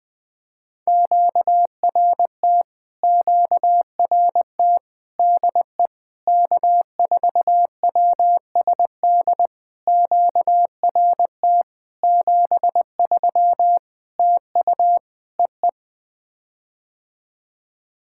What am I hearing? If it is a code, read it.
Morse code, heard as a keyed tone, about 20 wpm, 700 Hz. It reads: QRT QRT DE K4WSD QRT 73 TU EE
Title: POTA QSO Practice - 20wpm